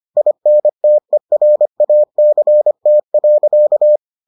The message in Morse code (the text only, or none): INTERACT.